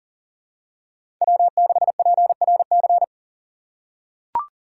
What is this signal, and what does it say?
Morse code, heard as a keyed tone, about 38 wpm, 700 Hz. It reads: W6PRC
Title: USA Call Signs - 4x Character Spacing - 40wpm